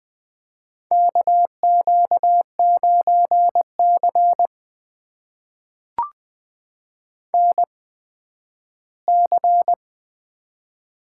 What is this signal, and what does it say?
Morse code, heard as a keyed tone, about 20 wpm, spaced as for 12 wpm, 700 Hz. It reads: KQ9C NC
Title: USA Call Signs - 4 Characters - 8x Character Spacing - 20wpm